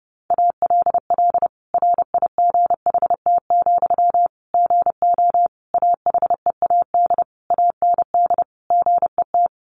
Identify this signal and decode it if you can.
Morse code, heard as a keyed tone, about 30 wpm, 700 Hz. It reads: ALL RIGHT, GO AHEAD AND GET
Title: Sentences from Top 1100 Words 30wpm